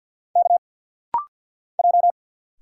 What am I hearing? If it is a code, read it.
Morse code, heard as a keyed tone, about 50 wpm, 700 Hz. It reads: K J
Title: Single Letters - Rapid-Fire 50wpm